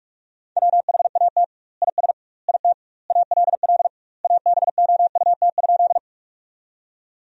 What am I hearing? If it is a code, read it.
Morse code, heard as a keyed tone, about 46 wpm, 700 Hz. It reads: WHAT IS IT ALL ABOUT?
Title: 2x Word-Spacing - Sentences from Top 100 Words - 45wpm